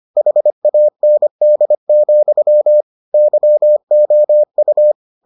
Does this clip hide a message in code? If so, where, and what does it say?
Yes: Morse code from 0.2 to 4.9 s, heard as a keyed tone, about 25 wpm, 600 Hz. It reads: HAND, YOU